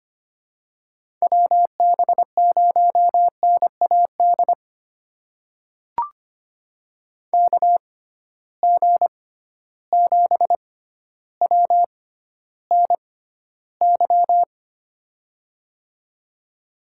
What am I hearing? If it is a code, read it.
Morse code, heard as a keyed tone, about 25 wpm, spaced as for 13 wpm, 700 Hz. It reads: WB0NAD KG7WNY
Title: USA Call Signs - 6 Characters - 6x Character Spacing - 25wpm